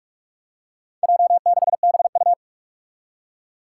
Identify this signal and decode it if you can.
Morse code, heard as a keyed tone, about 41 wpm, 700 Hz. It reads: J6BU
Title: Four Challenging Letter-Number Combinations 45wpm